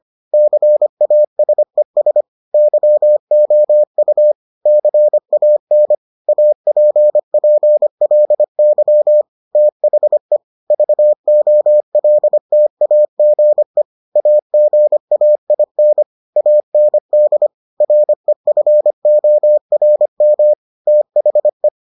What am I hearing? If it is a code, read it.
Morse code, heard as a keyed tone, about 25 wpm, 600 Hz. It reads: CASES YOU CAN APPLY THE VOLTAGE AGAIN AND REFORM THE